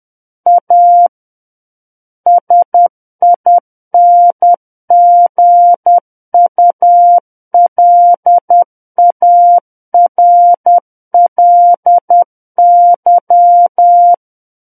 Morse code, heard as a keyed tone, about 10 wpm, 700 Hz. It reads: A SINGULARLY
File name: Book One / 01-Chapter 1 - 10 WPM